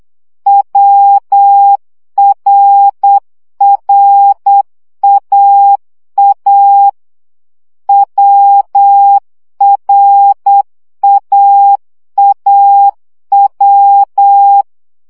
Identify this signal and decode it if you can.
Morse code, heard as a keyed tone, about 8 wpm, 800 Hz. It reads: WRRAA WRAAW
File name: CW gyakorlas a r w 7 baud